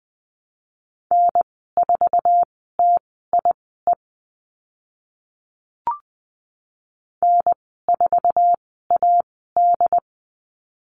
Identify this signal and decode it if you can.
Morse code, heard as a keyed tone, about 20 wpm, spaced as for 15 wpm, 700 Hz. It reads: N4TIE N4AD
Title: 082 Lesson 14g - Number 2 - Call Signs 20wpm